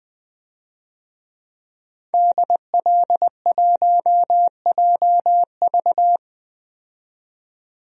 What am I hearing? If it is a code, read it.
Morse code, heard as a keyed tone, about 20 wpm, 700 Hz. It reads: DL1JV
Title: DXCC-100 Call Signs - 5 Characters - 20wpm